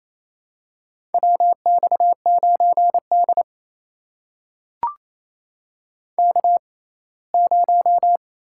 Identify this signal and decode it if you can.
Morse code, heard as a keyed tone, about 28 wpm, spaced as for 19 wpm, 700 Hz. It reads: WX9D K0
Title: USA Call Signs - 4 Characters - 6x Character Spacing - 28wpm